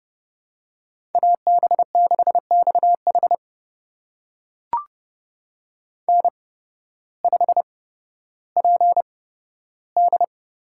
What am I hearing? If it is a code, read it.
Morse code, heard as a keyed tone, about 30 wpm, spaced as for 14 wpm, 700 Hz. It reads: AB6XH N5PD
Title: USA Call Signs - 5 Characters - 8x Character Spacing - 30wpm